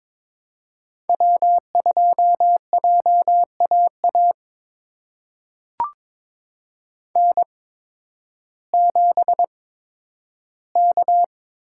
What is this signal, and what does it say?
Morse code, heard as a keyed tone, about 23 wpm, spaced as for 12 wpm, 700 Hz. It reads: W2JAA N7K
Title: USA Call Signs - 8x Character Spacing - 22wpm